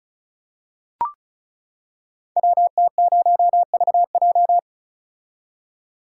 Morse code, heard as a keyed tone, about 36 wpm, 700 Hz. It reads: WT0VJ
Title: Code Groups - 5 Letter:Numbers - 35wpm